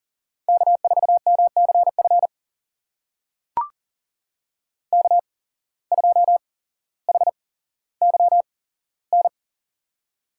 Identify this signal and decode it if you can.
Morse code, heard as a keyed tone, about 40 wpm, spaced as for 17 wpm, 700 Hz. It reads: K4MCF K2HYN
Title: USA Call Signs - 5 Characters - 8x Character Spacing - 40wpm